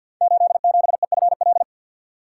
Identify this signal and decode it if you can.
Morse code, heard as a keyed tone, about 52 wpm, 700 Hz. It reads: 87EFL
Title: Code Groups - 5 Letter:Numbers - 50wpm